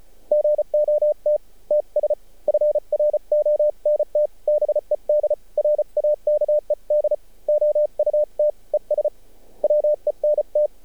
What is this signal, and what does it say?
Morse code, heard as a keyed tone, about 34 wpm, 600 Hz. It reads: GOT TS FRONT BED RAKED OUT ES WENT
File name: archived / CRSnet